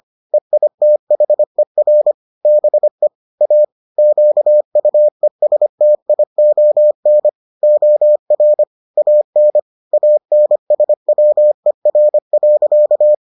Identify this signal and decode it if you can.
Morse code, heard as a keyed tone, about 25 wpm, 600 Hz. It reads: EITHER BE A QUESTION OR AN ANSWER.